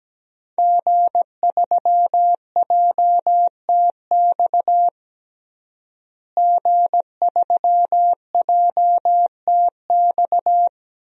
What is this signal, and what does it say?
Morse code, heard as a keyed tone, about 17 wpm, 700 Hz. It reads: G3JTX G3JTX
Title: DXCC-100 Call Signs - 5 Characters - Repeated 3x - 17wpm